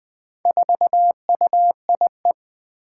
Morse code, heard as a keyed tone, about 20 wpm, 700 Hz. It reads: 4UIE
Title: Four Challenging Letter-Number Combinations 20wpm